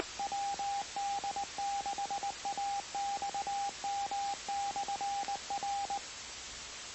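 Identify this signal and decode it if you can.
Morse code, heard as a keyed tone, about 20 wpm, 800 Hz, with radio noise. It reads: WD6AXM/R
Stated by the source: Recording on the 2 m band, 146.085 MHz.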